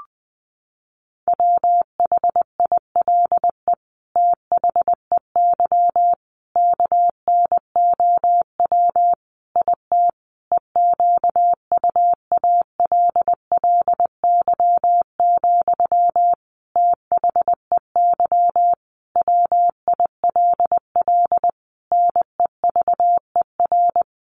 Morse code, heard as a keyed tone, about 19 wpm, 700 Hz. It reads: WHILE THEY KNOW IT EQUALLY, THEY WILL NEVER
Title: Sentences from Top 1400 Words 20wpm